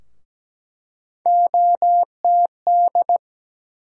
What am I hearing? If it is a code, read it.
Morse code, heard as a keyed tone, about 17 wpm, 700 Hz. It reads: OTD
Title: USA Call Sign Postfixes - 3 Letters 17wpm